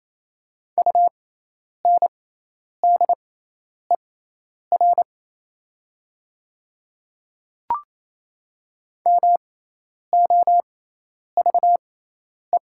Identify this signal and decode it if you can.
Morse code, heard as a keyed tone, about 28 wpm, spaced as for 10 wpm, 700 Hz. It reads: UNDER MOVE
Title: 6x Character Spacing - Top 300 Words - 28wpm